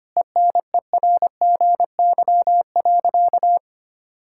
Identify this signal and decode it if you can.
Morse code, heard as a keyed tone, about 25 wpm, 700 Hz. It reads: ENERGY.
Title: Sentences from Top 1000 Words - Review Part 6 - 25wpm